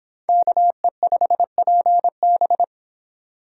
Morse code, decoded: KE5PB

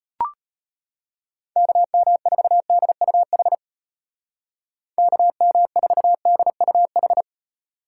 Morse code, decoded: KM4DUH KM4DUH

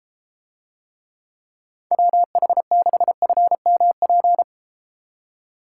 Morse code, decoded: WH6FMP

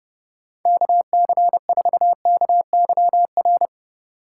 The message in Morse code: KC4KYR

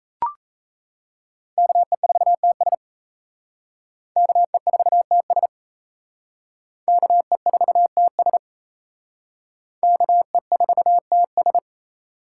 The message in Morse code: KE4TS KE4TS KE4TS KE4TS

